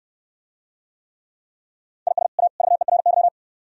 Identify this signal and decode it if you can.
Morse code, heard as a keyed tone, about 64 wpm, 700 Hz. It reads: IS A VERY